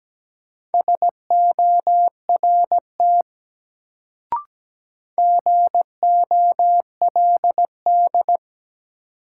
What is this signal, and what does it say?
Morse code, heard as a keyed tone, about 17 wpm, 700 Hz. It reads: SORT GOLD
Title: Sentences from Top 1000 Words - Review Part 3 - New Word Drill - 17wpm